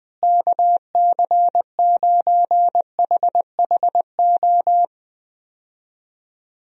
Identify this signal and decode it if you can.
Morse code, heard as a keyed tone, about 20 wpm, 700 Hz. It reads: KC9HHO
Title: USA Call Signs - 6 Characters - Repeated 2x - 20wpm